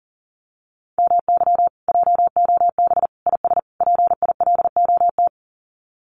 Morse code, heard as a keyed tone, about 40 wpm, 700 Hz. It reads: MY JOB IS PILOT
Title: Common QSO Phrases Encore 40wpm